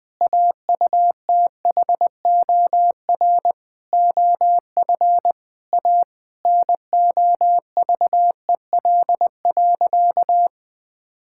Morse code, heard as a keyed tone, about 20 wpm, 700 Hz. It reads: AUTHOR OF A NOVEL.
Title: Sentences with 4-Word Phrases - 20wpm